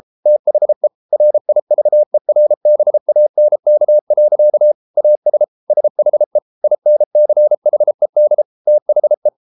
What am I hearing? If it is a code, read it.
Morse code, heard as a keyed tone, about 33 wpm, 600 Hz. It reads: THE RIVERBANK. AS SHE INCHED THE